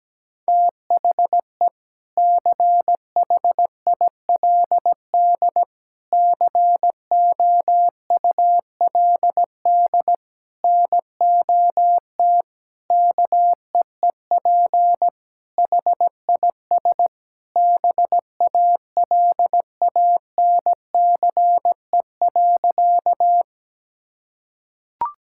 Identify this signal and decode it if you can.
Morse code, heard as a keyed tone, about 17 wpm, 700 Hz. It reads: THE CHILD COULD NOT KEEP HIS BALANCE.
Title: Sentences from Top 1000 Words - Review Part 6 - 17wpm